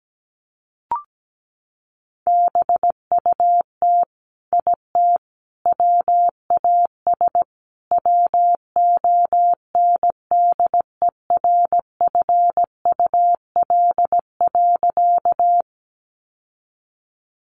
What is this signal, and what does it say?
Morse code, heard as a keyed tone, about 17 wpm, 700 Hz. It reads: BUT IT WAS WONDERFUL.
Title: Sentences from Top 1100 Words 17wpm